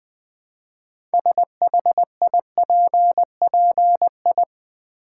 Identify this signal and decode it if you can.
Morse code, heard as a keyed tone, about 20 wpm, 700 Hz. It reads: SHIPPI
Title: Build A Word - 3 Letters At A Time - 8-Letter Words - 20wpm